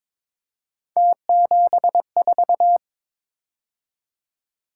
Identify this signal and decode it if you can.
Morse code, heard as a keyed tone, about 26 wpm, 700 Hz. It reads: T74
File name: Speed-Racing - Three Letter-Number Combinations - 22wpm